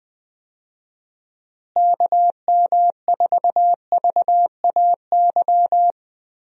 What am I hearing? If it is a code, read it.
Morse code, heard as a keyed tone, about 19 wpm, 700 Hz. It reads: KM4VAY
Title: USA Call Signs - 6 Characters - Repeated 2x - 20wpm